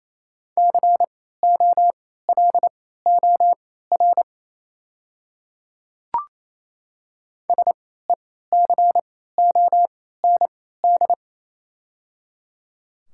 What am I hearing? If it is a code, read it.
Morse code, heard as a keyed tone, about 28 wpm, spaced as for 16 wpm, 700 Hz. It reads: COLOR SECOND
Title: 3x Character Spacing - Top 300 Words - 28wpm